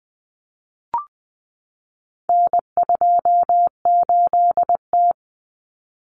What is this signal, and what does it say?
Morse code, heard as a keyed tone, about 20 wpm, 700 Hz. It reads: N28T
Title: Common Cut Numbers - 4-Digit 20wpm